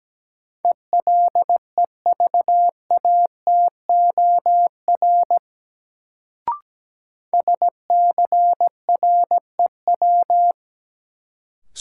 Morse code, heard as a keyed tone, about 17 wpm, 700 Hz. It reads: ELEVATOR SCREW
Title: Sentences from Top 1600 Words - New Word Drill Alt - 17wpm